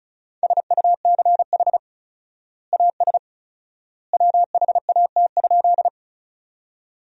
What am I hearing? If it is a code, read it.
Morse code, heard as a keyed tone, about 34 wpm, 700 Hz. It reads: SUCH AS WHAT?
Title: 4x Word-Spacing - Sentences from Top 200 Words - 35wpm